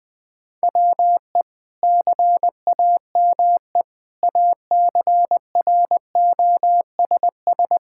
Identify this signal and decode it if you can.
Morse code, heard as a keyed tone, about 20 wpm, 700 Hz. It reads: WE CAME ACROSS
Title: Sentences from Top 1000 Words - Review Part 5 - 20wpm